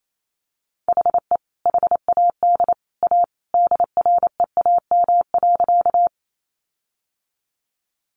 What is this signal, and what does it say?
Morse code, heard as a keyed tone, about 28 wpm, 700 Hz. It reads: HE HAD A DREAM.
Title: Sentences from Top 1500 Words 28wpm